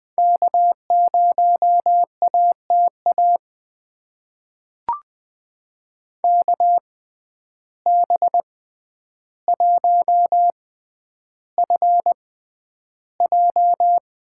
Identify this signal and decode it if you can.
Morse code, heard as a keyed tone, about 20 wpm, spaced as for 10 wpm, 700 Hz. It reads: K0ATA KB1FJ
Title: USA Call Signs - 6x Character Spacing - 20wpm